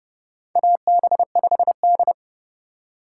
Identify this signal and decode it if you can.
Morse code, heard as a keyed tone, about 30 wpm, 700 Hz. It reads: AB5D
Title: USA Call Signs - 3x Character Spacing - 30wpm